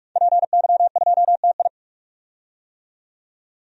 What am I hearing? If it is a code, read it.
Morse code, heard as a keyed tone, about 46 wpm, 700 Hz. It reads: PY2TI
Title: DXCC-100 Call Signs - 5 Characters - 45wpm